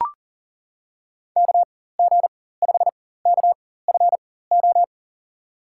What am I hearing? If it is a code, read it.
Morse code, heard as a keyed tone, about 40 wpm, spaced as for 19 wpm, 700 Hz. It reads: KG5KFO
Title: USA Call Signs - 6 Characters - 4x Character Spacing - 40wpm